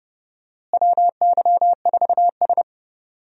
Morse code, heard as a keyed tone, about 31 wpm, 700 Hz. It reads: WY4S
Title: USA Call Signs - 4 Characters - Repeated 2x - 30wpm